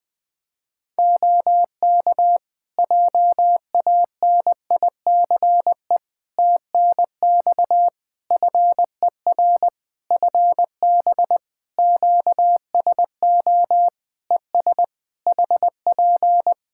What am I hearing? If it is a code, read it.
Morse code, heard as a keyed tone, about 21 wpm, 700 Hz. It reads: OK JANICE TNX FER FB QSO ES HP
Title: K1USN SST QSO Practice 20wpm